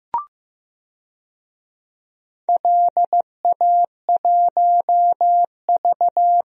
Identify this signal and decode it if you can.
Morse code, heard as a keyed tone, about 15 wpm, 700 Hz. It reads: LA1V
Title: DXCC-100 Call Signs - 4 Characters - Repeated 2x - 15wpm